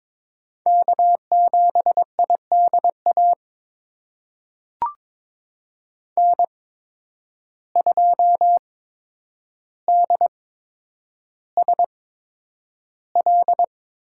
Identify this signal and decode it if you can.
Morse code, heard as a keyed tone, about 22 wpm, spaced as for 9 wpm, 700 Hz. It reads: K7IDA N2DSL
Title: USA Call Signs - 5 Characters - 8x Character Spacing - 22wpm